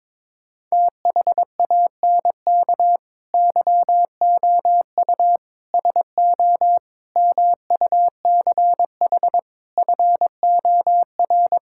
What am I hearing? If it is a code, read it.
Morse code, heard as a keyed tone, about 22 wpm, 700 Hz. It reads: THANK YOU SO MUCH FOR
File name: practice / 22wpm / Sentences with 4-Word Phrases - 22wpm